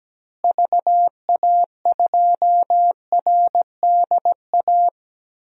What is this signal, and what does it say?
Morse code, heard as a keyed tone, about 17 wpm, 700 Hz. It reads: VA2RDA
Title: DXCC-100 Call Signs - 6 Characters - Repeated 3x - 17wpm